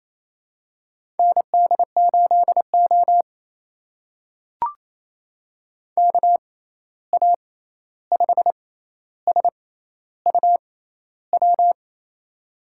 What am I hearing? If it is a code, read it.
Morse code, heard as a keyed tone, about 28 wpm, spaced as for 13 wpm, 700 Hz. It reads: ND8O KA5SUW
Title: USA Call Signs - 6x Character Spacing - 28wpm